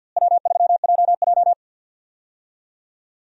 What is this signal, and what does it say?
Morse code, heard as a keyed tone, about 52 wpm, 700 Hz. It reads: W3JJ
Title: USA Call Signs - 4 Characters - Repeated 3x - 50wpm